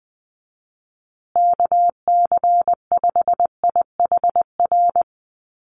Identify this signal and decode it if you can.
Morse code, heard as a keyed tone, about 20 wpm, 700 Hz. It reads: KC5IHR
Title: USA Call Signs Repeated 3x 20wpm